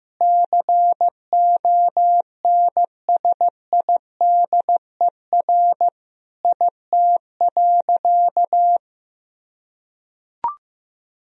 Morse code, heard as a keyed tone, about 15 wpm, 700 Hz. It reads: CONSIDER IT.
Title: Sentences from Top 1000 Words - Review Part 4 - 15wpm